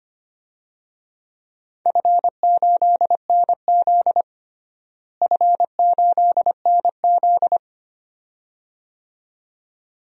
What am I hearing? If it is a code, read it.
Morse code, heard as a keyed tone, about 25 wpm, 700 Hz. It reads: F8NZ F8NZ
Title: DXCC-100 Call Signs - 4 Characters - Repeated 2x - 25wpm